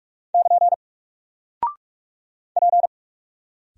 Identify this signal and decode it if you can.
Morse code, heard as a keyed tone, about 44 wpm, 700 Hz. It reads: <KN> P
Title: Single Character (Letter:Number:Punctuation:Prosign) 45wpm